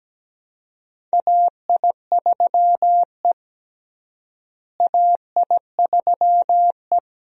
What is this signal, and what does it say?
Morse code, heard as a keyed tone, about 17 wpm, 700 Hz. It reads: AI3E AI3E
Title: USA Call Signs - 4 Characters - Repeated 3x - 17wpm